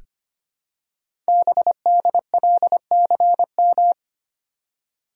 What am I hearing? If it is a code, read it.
Morse code, heard as a keyed tone, about 25 wpm, 700 Hz. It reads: BDLCM
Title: Code Groups - 5 Letters - 25wpm